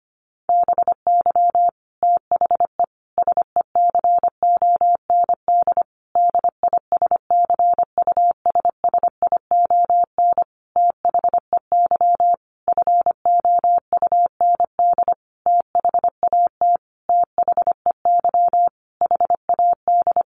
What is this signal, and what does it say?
Morse code, heard as a keyed tone, about 25 wpm, 700 Hz. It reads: BY THE SECOND DISCUSSION THEY FOUND THAT THEY HAD
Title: Sentences from Top 1200 Words 25wpm